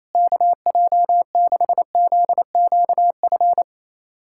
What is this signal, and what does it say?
Morse code, heard as a keyed tone, about 27 wpm, 700 Hz. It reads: KJ6ZQF
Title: USA Callsign DE Callsign 28wpm